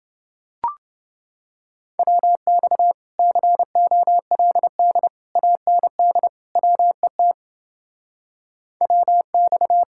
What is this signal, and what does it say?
Morse code, heard as a keyed tone, about 30 wpm, 700 Hz. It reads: WX COLD AND WET WX